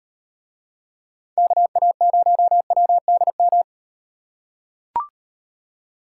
Morse code, heard as a keyed tone, about 37 wpm, 700 Hz. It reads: KA0WDM